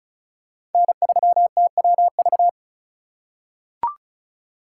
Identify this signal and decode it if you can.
Morse code, heard as a keyed tone, about 36 wpm, 690 Hz. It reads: N3TWV
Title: USA Call Signs - 3x Character Spacing - 35wpm